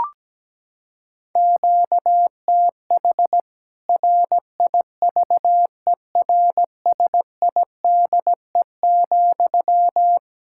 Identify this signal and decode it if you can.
Morse code, heard as a keyed tone, about 17 wpm, 700 Hz. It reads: QTH RIVERSIDE,
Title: Practice QSO - Exchange 1 - Beginning - 17wpm